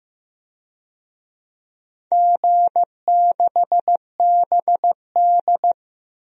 Morse code, heard as a keyed tone, about 15 wpm, 700 Hz. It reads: G6BD
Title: DXCC-100 Call Signs - 4 Characters - 15wpm